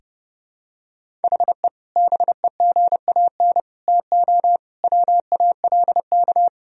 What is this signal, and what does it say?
Morse code, heard as a keyed tone, about 30 wpm, 700 Hz. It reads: HE BEGAN TO WALK
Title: Sentences from Top 1000 Words - Review Part 1 - 30wpm